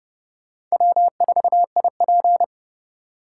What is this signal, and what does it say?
Morse code, heard as a keyed tone, about 37 wpm, 700 Hz. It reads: W4IP
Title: Speed Racing - USA 1x2 Call Signs 30wpm